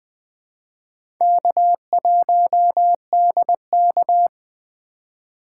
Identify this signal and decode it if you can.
Morse code, heard as a keyed tone, about 20 wpm, 700 Hz. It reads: K1DK
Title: USA Call Signs - 4 Characters - 2x Character Spacing - 20wpm